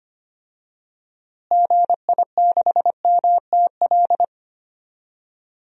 Morse code, heard as a keyed tone, about 25 wpm, 700 Hz. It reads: GI6MTL
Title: DXCC-100 Call Signs - 6 Characters - Repeated 3x - 25wpm